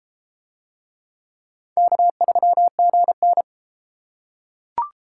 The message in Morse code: K3GN

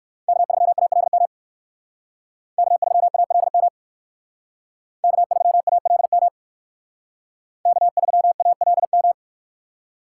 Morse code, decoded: K3ALM K3ALM K3ALM K3ALM

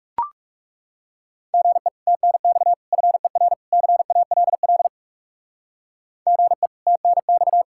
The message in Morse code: GE TNX FER CALL GE TNX

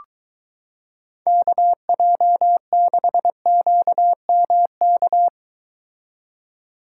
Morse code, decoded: KJ6QMK

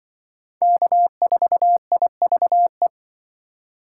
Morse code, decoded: K4IVE

